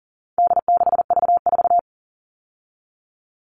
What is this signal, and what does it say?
Morse code, heard as a keyed tone, about 40 wpm, 700 Hz. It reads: D6V4
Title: All Cut Numbers - 4-Digit 40wpm